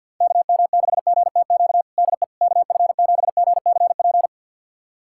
Morse code, heard as a keyed tone, about 50 wpm, 700 Hz. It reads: KM6GTQ DE KF7GCP
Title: USA Callsign DE Callsign - 6 Characters - 50wpm